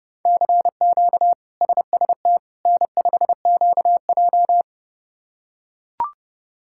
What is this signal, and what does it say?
Morse code, heard as a keyed tone, about 30 wpm, 700 Hz. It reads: CQ SST N5QJ
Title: K1USN SST QSO Practice - 30wpm